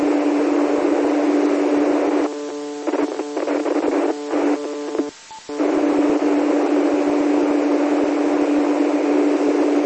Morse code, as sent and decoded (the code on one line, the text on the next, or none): none